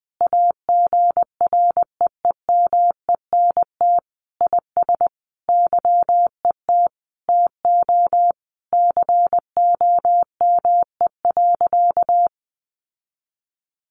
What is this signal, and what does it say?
Morse code, heard as a keyed tone, about 20 wpm, 700 Hz. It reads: AGREEMENT IS YET TO COME.
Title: Sentences from Top 1500 Words 20wpm